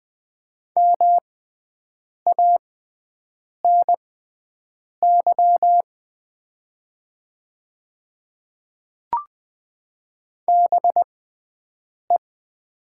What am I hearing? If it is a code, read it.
Morse code, heard as a keyed tone, about 20 wpm, spaced as for 7 wpm, 700 Hz. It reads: MANY BE
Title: 6x Character Spacing - Top 100 Words - 20wpm